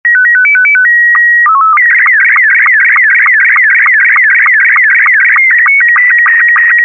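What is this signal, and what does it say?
SSTV (slow-scan television), Robot 72.
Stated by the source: Robot72 mode SSTV transmission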